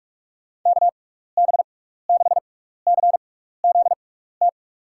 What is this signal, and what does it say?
Morse code, heard as a keyed tone, about 43 wpm, spaced as for 15 wpm, 700 Hz. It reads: KB6CZT
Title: USA Call Signs - 6 Characters - 6x Character Spacing - 45wpm